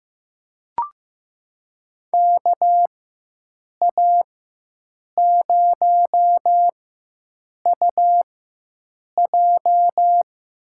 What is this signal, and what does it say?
Morse code, heard as a keyed tone, about 15 wpm, spaced as for 7 wpm, 700 Hz. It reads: KA0UJ
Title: USA Call Signs - 4x Character Spacing - 15wpm